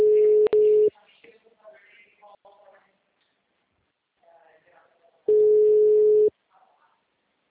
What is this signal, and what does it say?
Ringback tone (Europe, 425 Hz), with speech.
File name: out- -2007- - - .65657